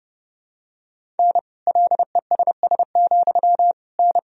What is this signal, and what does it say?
Morse code, heard as a keyed tone, about 30 wpm, 700 Hz. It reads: N LESS, N
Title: Sentences from Top 400 Words and Abbreviations 30wpm